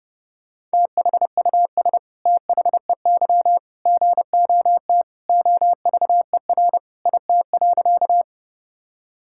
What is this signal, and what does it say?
Morse code, heard as a keyed tone, about 30 wpm, 700 Hz. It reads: THUS THEY GOT OVER IT.
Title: Sentences from Top 1000 Words - Review Part 5 - 30wpm